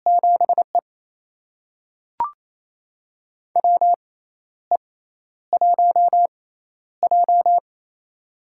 Morse code, heard as a keyed tone, about 28 wpm, spaced as for 12 wpm, 700 Hz. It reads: 7E WE1J
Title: USA Call Signs - 4 Characters - 6x Character Spacing - 28wpm